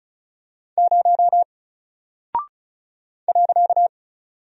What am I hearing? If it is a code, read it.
Morse code, heard as a keyed tone, about 36 wpm, 700 Hz. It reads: 0 .
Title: Single Character (Letter:Number:Punctuation:Prosign) Alt 35wpm